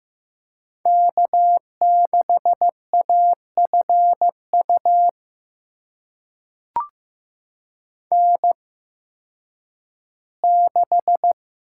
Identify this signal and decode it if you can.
Morse code, heard as a keyed tone, about 15 wpm, spaced as for 10 wpm, 700 Hz. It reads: K6AFU N6
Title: USA Call Signs - 8x Character Spacing - 15wpm